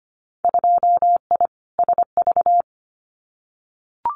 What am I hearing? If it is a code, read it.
Morse code, heard as a keyed tone, about 25 wpm, 700 Hz. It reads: 2I SV
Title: Winter Field Day Exchanges Encore 25wpm